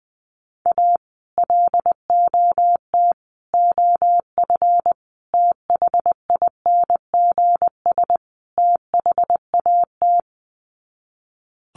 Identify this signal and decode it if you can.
Morse code, heard as a keyed tone, about 20 wpm, 700 Hz. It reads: A LOT OF THINGS THAT
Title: Sets of 5 Words 20wpm